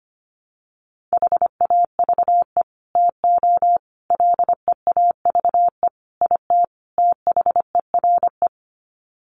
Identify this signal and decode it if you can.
Morse code, heard as a keyed tone, about 25 wpm, 700 Hz. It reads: HAVE TO LEAVE IT THERE
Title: Sets of 5 Words Encore2 25wpm